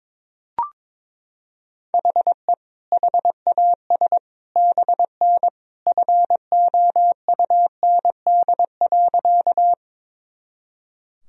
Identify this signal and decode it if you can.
Morse code, heard as a keyed tone, about 22 wpm, 700 Hz. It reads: HE HAS BN FOUND.
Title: Sentences from Top 300 Words and Abbreviations 22wpm